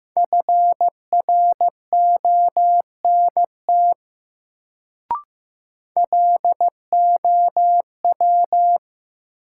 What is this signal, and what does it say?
Morse code, heard as a keyed tone, about 15 wpm, 700 Hz. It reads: FRONT LOW
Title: Sentences from Top 500 Words - New Word Drill - 15wpm